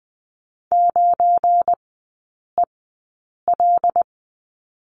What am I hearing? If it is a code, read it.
Morse code, heard as a keyed tone, about 19 wpm, spaced as for 8 wpm, 690 Hz. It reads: 9EL
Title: 205 Lesson 33e - Letter j - Three Characters 20wpm